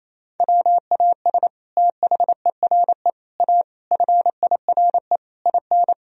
Morse code, heard as a keyed tone, about 28 wpm, 700 Hz. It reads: WAS THERE A FIRE IN
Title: Sentences from Top 1000 Words - Review Part 4 - 28wpm